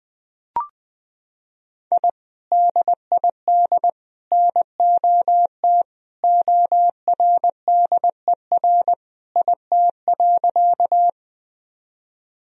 Morse code, heard as a keyed tone, about 20 wpm, 700 Hz. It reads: I DID NOT ORDER IT.
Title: Sentences from Top 1000 Words - Review Part 2 - 20wpm